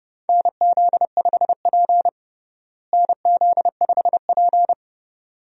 Morse code, heard as a keyed tone, about 32 wpm, 700 Hz. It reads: NZ5P NZ5P
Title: USA Call Signs - 4 Characters - Repeated 2x - 30wpm